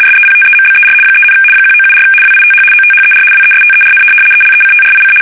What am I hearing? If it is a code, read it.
RTTY, about 75 baud. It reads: SEPTEMBER 3, 1974 WITH OTHER WESTERN ELECTRIC
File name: fsk850-75B / teletype-corp